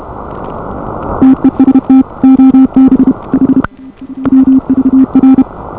Morse code, heard as a keyed tone, about 29 wpm, 280 Hz. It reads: TEST OB H WVR